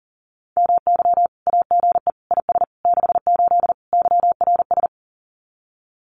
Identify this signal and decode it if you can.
Morse code, heard as a keyed tone, about 40 wpm, 700 Hz. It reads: MY AGE IS 68 YRS
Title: Common QSO Phrases Encore 40wpm